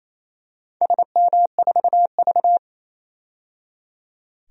Morse code, heard as a keyed tone, about 28 wpm, 700 Hz. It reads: SM4V
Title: DXCC-100 Call Signs - 4 Characters - 28wpm